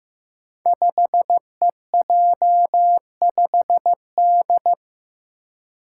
Morse code, heard as a keyed tone, about 15 wpm, 700 Hz. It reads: 5EJ5D
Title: Five Challenging Letter-Number Combinations 15wpm